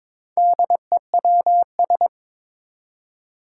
Morse code, heard as a keyed tone, about 21 wpm, 700 Hz. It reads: DEWS
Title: Four Challenging Letter-Number Combinations 22wpm